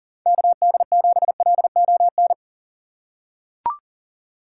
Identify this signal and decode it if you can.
Morse code, heard as a keyed tone, about 40 wpm, 700 Hz. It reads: KD7LON
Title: USA Call Signs - 4x Character Spacing - 40wpm